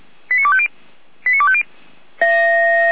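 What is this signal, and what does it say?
Five-tone selective call (ZVEI), heard as quick runs of five tones, tones of about 70 ms. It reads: 89250 89250
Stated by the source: ZVEI five-tone fire brigade call received on SDR.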